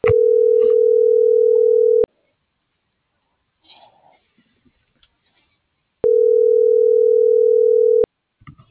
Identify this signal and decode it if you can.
Ringback tone (North America), 440 + 480 Hz.